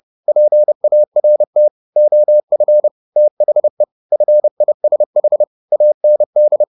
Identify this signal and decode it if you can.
Morse code, heard as a keyed tone, about 30 wpm, 600 Hz. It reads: PART OF THE FISH AND